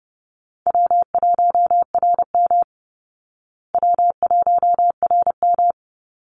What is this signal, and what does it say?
Morse code, heard as a keyed tone, about 30 wpm, 700 Hz. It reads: W1RM W1RM
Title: Easy Call Signs Repeated 3x - 30wpm